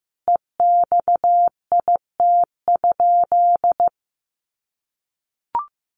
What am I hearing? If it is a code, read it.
Morse code, heard as a keyed tone, about 15 wpm, 700 Hz. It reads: EXIT?
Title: Sentences from Top 1100 Words 15wpm